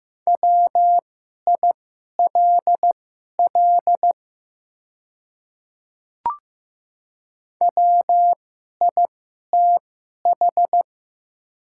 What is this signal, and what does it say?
Morse code, heard as a keyed tone, about 15 wpm, spaced as for 11 wpm, 700 Hz. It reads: WILL WITH
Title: 2x Character Spacing - Top 100 Words - 15wpm